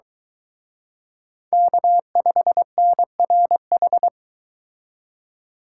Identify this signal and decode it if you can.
Morse code, heard as a keyed tone, about 22 wpm, 700 Hz. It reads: K5NRH